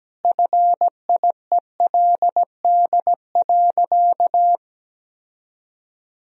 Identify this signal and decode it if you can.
Morse code, heard as a keyed tone, about 17 wpm, 700 Hz. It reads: FIELD.
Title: Sentences from Top 1000 Words - Review Part 2 - 17wpm